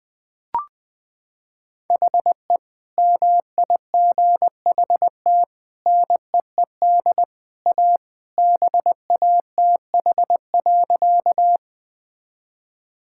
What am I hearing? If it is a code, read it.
Morse code, heard as a keyed tone, about 20 wpm, 700 Hz. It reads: HE MIGHT NEED A BATH.
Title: Sentences from Top 1600 Words 20wpm